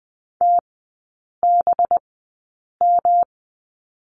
Morse code, heard as a keyed tone, about 20 wpm, spaced as for 8 wpm, 700 Hz. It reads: TBM
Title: 184 Lesson 30e - Letter b - Three Characters 20wpm